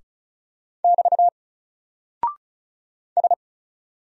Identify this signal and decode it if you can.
Morse code, heard as a keyed tone, about 34 wpm, 700 Hz. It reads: <BT> S
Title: Single Character (Letter:Number:Punctuation:Prosign) 35wpm